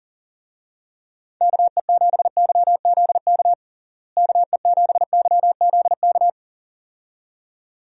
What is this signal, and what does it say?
Morse code, heard as a keyed tone, about 40 wpm, 700 Hz. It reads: KE7YZK KE7YZK
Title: USA Call Signs - 6 Characters - Repeated 2x - 40wpm